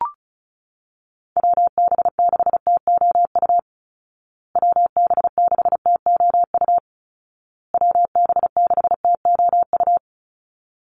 Morse code, heard as a keyed tone, about 34 wpm, 700 Hz. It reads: WB6TOU WB6TOU WB6TOU
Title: Call Signs Repeated 2x - 35wpm